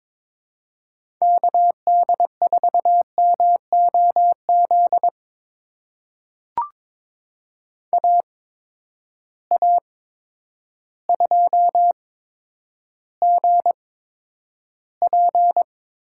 Morse code, heard as a keyed tone, about 22 wpm, spaced as for 10 wpm, 700 Hz. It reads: KD4MOZ AA2GP
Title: USA Call Signs - 8x Character Spacing - 22wpm